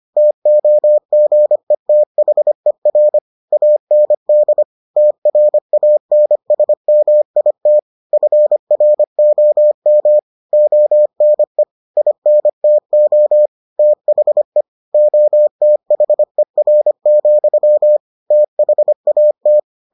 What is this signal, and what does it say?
Morse code, heard as a keyed tone, about 25 wpm, 600 Hz. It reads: TOGETHER AND TRANSMIT FROM ONE INTO THE OTHER, THAT